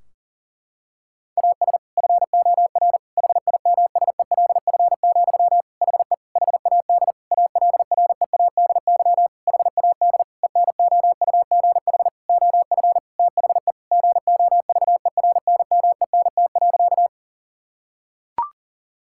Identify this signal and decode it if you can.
Morse code, heard as a keyed tone, about 37 wpm, 700 Hz. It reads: AS FOR HIMSELF, HE HAD ALREADY HAD ENOUGH OF THE GOVERNMENT.
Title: Sentences from Top 1600 Words 40wpm